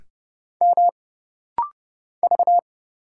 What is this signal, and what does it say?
Morse code, heard as a keyed tone, about 30 wpm, 700 Hz. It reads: M V
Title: Single Letter-Number Alt - Rapid-Fire 30wpm